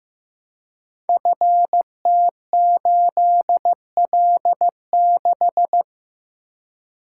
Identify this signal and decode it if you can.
Morse code, heard as a keyed tone, about 15 wpm, 700 Hz. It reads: FT8L6
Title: Code Groups - 5 Letter:Numbers - 15wpm